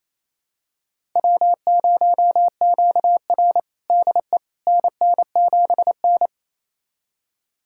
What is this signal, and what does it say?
Morse code, heard as a keyed tone, about 27 wpm, 700 Hz. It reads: W0QR DE NN7N
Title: USA Callsign DE Callsign - 4 Characters - 28wpm